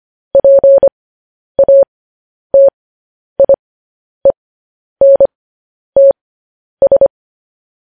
Morse code, heard as a keyed tone, about 25 wpm, spaced as for 10 wpm, 550 Hz. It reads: PATIENTS
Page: news-headlines-in-morse-code-at-10-wpm